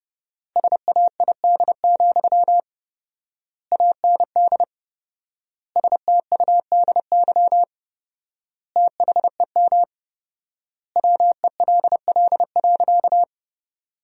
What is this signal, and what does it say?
Morse code, heard as a keyed tone, about 30 wpm, 700 Hz. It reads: SAID, AND STUDY THEM WELL.
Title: 4x Word-Spacing - Sentences from Top 300 Words - 30wpm